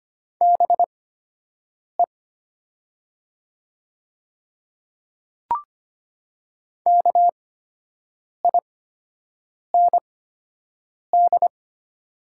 Morse code, heard as a keyed tone, about 25 wpm, spaced as for 7 wpm, 700 Hz. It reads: BE KIND
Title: 8x Character Spacing - Top 200 Words - 25wpm